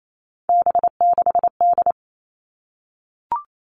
Morse code, heard as a keyed tone, about 28 wpm, 700 Hz. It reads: B6D
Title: All Cut Numbers - 3-Digit 28wpm